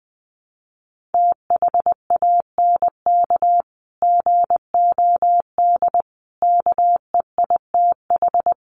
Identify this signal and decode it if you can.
Morse code, heard as a keyed tone, about 20 wpm, 700 Hz. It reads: THANK GOD KEITH
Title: A Sound Of Thunder 20wpm